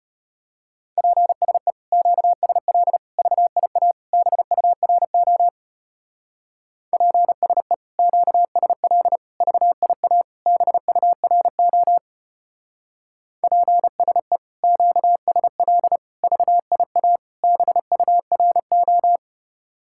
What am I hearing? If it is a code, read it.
Morse code, heard as a keyed tone, about 32 wpm, 700 Hz. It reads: PSE QSL VIA BURO PSE QSL VIA BURO PSE QSL VIA BURO